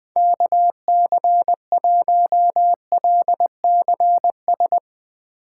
Morse code, heard as a keyed tone, about 20 wpm, 700 Hz. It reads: KC1LCS
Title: USA Call Signs - 6 Characters - 3x Character Spacing - 20wpm